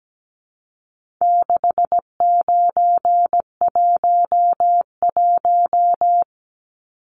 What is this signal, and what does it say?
Morse code, heard as a keyed tone, about 17 wpm, 700 Hz. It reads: 6911
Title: Numbers - 4-Digit 17wpm